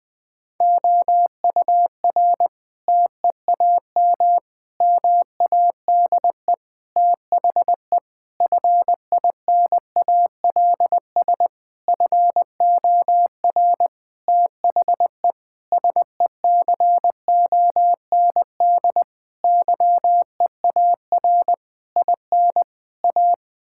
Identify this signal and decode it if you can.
Morse code, heard as a keyed tone, about 20 wpm, 700 Hz. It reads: OUR TEAM MADE THE FINALS FOR THE SECOND YEAR IN A
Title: Sentences with 7-Word Phrases - Spoken Beginning - 20wpm